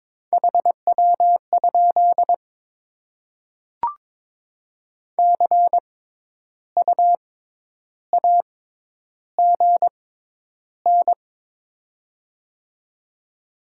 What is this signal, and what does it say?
Morse code, heard as a keyed tone, about 22 wpm, spaced as for 10 wpm, 700 Hz. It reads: HW? CU AGN
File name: Top 20 QSO Elements - 6x Character Spacing - 22wpm